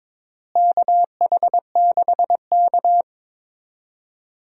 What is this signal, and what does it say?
Morse code, heard as a keyed tone, about 22 wpm, 700 Hz. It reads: KH6K
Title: DXCC-100 Call Signs - 4 Characters - Repeated 3x - 22wpm